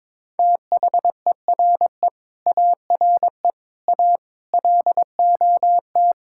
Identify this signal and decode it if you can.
Morse code, heard as a keyed tone, about 22 wpm, 700 Hz. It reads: THERE ARE A LOT
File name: Sentences with 7-Word Phrases - 22wpm